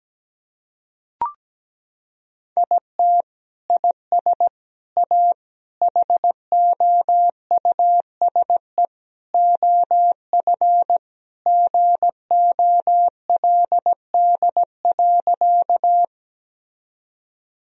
Morse code, heard as a keyed tone, about 17 wpm, 700 Hz. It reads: IT IS A HOUSE OF GOLD.
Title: Sentences from Top 1000 Words - Review Part 3 - 17wpm